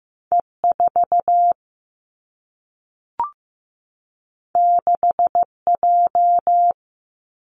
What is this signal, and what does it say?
Morse code, heard as a keyed tone, about 15 wpm, 700 Hz. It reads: E 4 6 J
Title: Two Challenging Letter-Number Combinations 15wpm